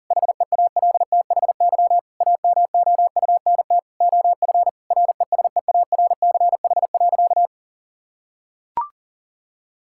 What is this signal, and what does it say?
Morse code, heard as a keyed tone, about 40 wpm, 700 Hz. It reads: HEALTHY AMOUNT OF RESEARCH.
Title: Sentences from Top 1000 Words - Review Part 5 - 40wpm